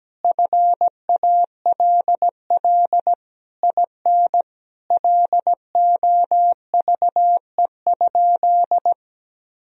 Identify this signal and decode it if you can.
Morse code, heard as a keyed tone, about 17 wpm, 700 Hz. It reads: FALL IN LOVE?
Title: Sentences from Top 1000 Words - Review Part 2 - 17wpm